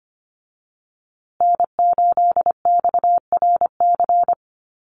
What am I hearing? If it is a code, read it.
Morse code, heard as a keyed tone, about 25 wpm, 700 Hz. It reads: N8XRC
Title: USA Call Signs Repeated 3x 25wpm